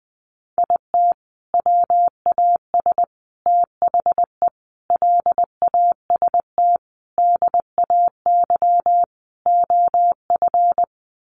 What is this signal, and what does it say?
Morse code, heard as a keyed tone, about 20 wpm, 700 Hz. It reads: IT WAS THE LAST DAY OF
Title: Sentences from Top 300 Words 20wpm